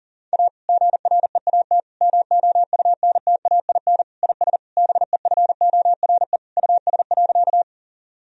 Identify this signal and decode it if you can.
Morse code, heard as a keyed tone, about 40 wpm, 700 Hz. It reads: A GREAT MOUNTAIN IS BEFORE US.
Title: Sentences from Top 1000 Words - Review Part 1 - 40wpm